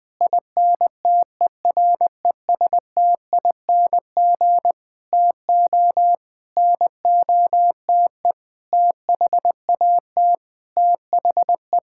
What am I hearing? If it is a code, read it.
Morse code, heard as a keyed tone, about 20 wpm, 700 Hz. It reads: INTERESTING TO NOTE THAT THE
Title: Sentences with 6-Word Phrases - 20wpm